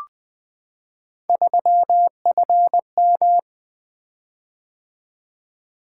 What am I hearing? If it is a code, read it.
Morse code, heard as a keyed tone, about 20 wpm, 700 Hz. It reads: 3FM
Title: Three Letter-Number Combinations - 20wpm